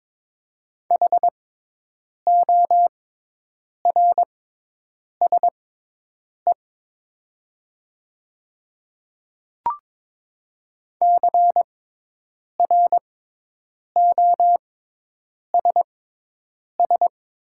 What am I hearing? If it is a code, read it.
Morse code, heard as a keyed tone, about 22 wpm, spaced as for 8 wpm, 700 Hz. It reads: HORSE CROSS
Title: 6x Character Spacing - Top 300 Words - 22wpm